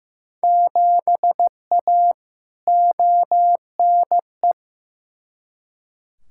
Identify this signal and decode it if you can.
Morse code, heard as a keyed tone, about 15 wpm, 700 Hz. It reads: 7A ONE
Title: Field Day Exchanges Encore - 15wpm